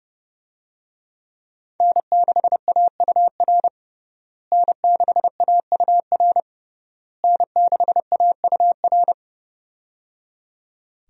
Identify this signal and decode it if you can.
Morse code, heard as a keyed tone, about 30 wpm, 700 Hz. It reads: N6AUR N6AUR N6AUR
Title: USA Call Signs - 5 Characters - Repeated 3x - 30wpm